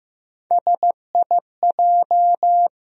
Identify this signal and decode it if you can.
Morse code, heard as a keyed tone, about 15 wpm, 700 Hz. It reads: SIJ
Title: USA Call Sign Postfixes - 3 Letters 15wpm